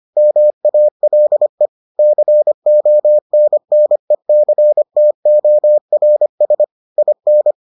Morse code, heard as a keyed tone, about 25 wpm, 600 Hz. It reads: MALE CONNECTORS IN